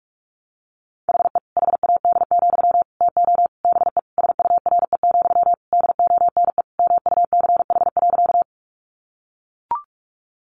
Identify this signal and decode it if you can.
Morse code, heard as a keyed tone, about 44 wpm, 700 Hz. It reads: HE HAD, TO BE SURE, DONE MUCH.
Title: Sentences from Top 500 Words 45wpm